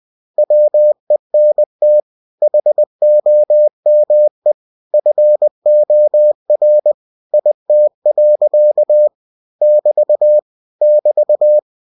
Morse code, heard as a keyed tone, about 20 wpm, 600 Hz. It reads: WENT HOME FOR IT. <BT> <BT>
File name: cw-books / winnie-the-pooh / winnie-20wpm-page-0025